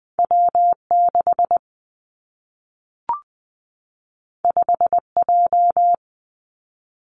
Morse code, heard as a keyed tone, about 20 wpm, 700 Hz. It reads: W 6 5 J
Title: Two Challenging Letter-Number Combinations 20wpm